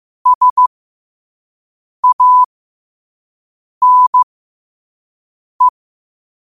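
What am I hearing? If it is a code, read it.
Morse code, heard as a keyed tone, about 15 wpm, spaced as for 5 wpm, 1000 Hz. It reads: SANE
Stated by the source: Morse code quote-of-the-day practice at 5 wpm.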